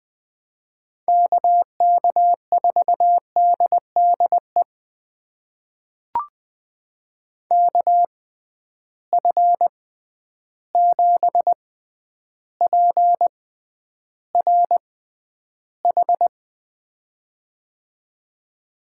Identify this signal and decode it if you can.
Morse code, heard as a keyed tone, about 20 wpm, spaced as for 10 wpm, 700 Hz. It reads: KK4DDE KF7PRH
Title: USA Call Signs - 6x Character Spacing - 20wpm